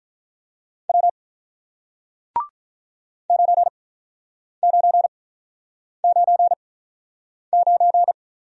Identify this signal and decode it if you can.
Morse code, heard as a keyed tone, about 48 wpm, spaced as for 9 wpm, 700 Hz. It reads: W 9999